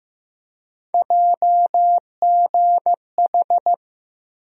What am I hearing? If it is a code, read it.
Morse code, heard as a keyed tone, about 15 wpm, 700 Hz. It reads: JGH
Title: USA Call Sign Postfixes - 3 Letters 15wpm